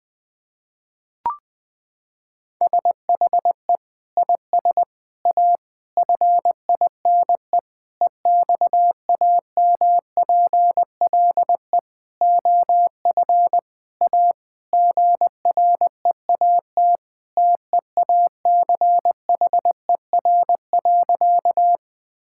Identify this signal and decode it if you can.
Morse code, heard as a keyed tone, about 20 wpm, 700 Hz. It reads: SHE IS A FINE EXAMPLE OF A GREAT TEACHER.
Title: Sentences from Top 1000 Words - Review Part 6 - 20wpm